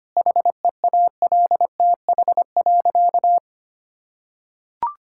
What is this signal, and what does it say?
Morse code, heard as a keyed tone, about 25 wpm, 700 Hz. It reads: HEALTH.
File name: practice / 25wpm / Sentences with 3-Word Phrases - 25wpm